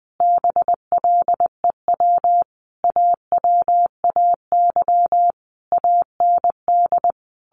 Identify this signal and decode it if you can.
Morse code, heard as a keyed tone, about 20 wpm, 700 Hz. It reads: BLEW AWAY AND
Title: A Sound Of Thunder 20wpm